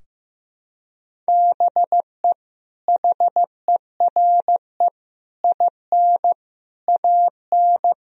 Morse code, heard as a keyed tone, about 16 wpm, 700 Hz. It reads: BE HERE IN AN
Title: Sentences from Top 1000 Words - Review Part 4 - 15wpm